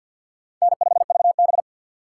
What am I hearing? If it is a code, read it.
Morse code, heard as a keyed tone, about 51 wpm, 690 Hz. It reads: N5VB
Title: USA Call Signs - 4 Characters - 6x Character Spacing - 50wpm